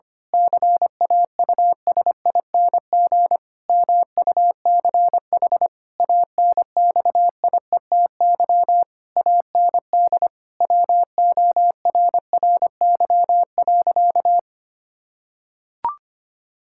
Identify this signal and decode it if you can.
Morse code, heard as a keyed tone, about 25 wpm, 700 Hz. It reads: CAUSING MUCH ANXIETY AND WORRY.
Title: Pangram Challenge v2 - 25wpm